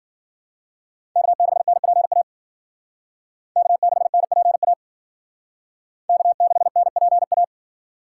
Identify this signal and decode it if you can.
Morse code, heard as a keyed tone, about 50 wpm, 700 Hz. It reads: K6NPA K6NPA K6NPA
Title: Speed Racing - USA 1x3 Call Signs 40wpm